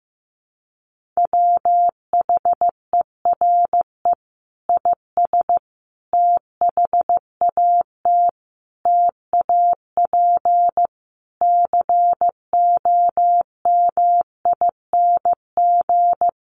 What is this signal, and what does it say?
Morse code, heard as a keyed tone, about 15 wpm, 700 Hz. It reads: WHERE IS THAT TAP COMING
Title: Sentences from Top 1400 Words 15wpm